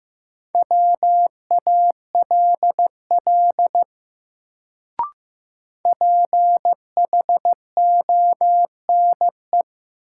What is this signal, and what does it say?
Morse code, heard as a keyed tone, about 16 wpm, 700 Hz. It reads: WALL PHONE
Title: Sentences from Top 1100 Words - New Word Drill Alt - 15wpm